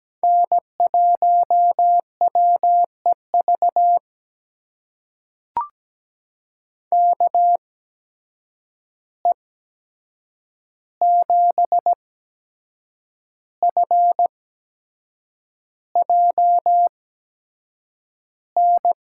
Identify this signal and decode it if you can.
Morse code, heard as a keyed tone, about 17 wpm, spaced as for 7 wpm, 700 Hz. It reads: N1WEV KE7FJN
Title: USA Call Signs - 8x Character Spacing - 17wpm